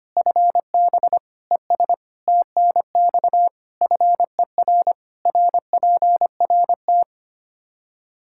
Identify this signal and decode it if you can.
Morse code, heard as a keyed tone, about 25 wpm, 700 Hz. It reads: FB ES TNX FER RPRT
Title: Practice QSO - Exchange 2 - Info - 25wpm